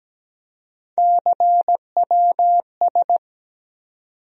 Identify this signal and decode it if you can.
Morse code, heard as a keyed tone, about 17 wpm, 700 Hz. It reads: CWS
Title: USA Call Sign Postfixes - 3 Letters 17wpm